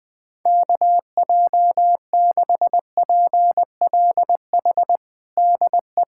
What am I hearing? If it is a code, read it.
Morse code, heard as a keyed tone, about 20 wpm, 700 Hz. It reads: KJ6PLH DE
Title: USA Callsign DE Callsign - 6 Characters - 20wpm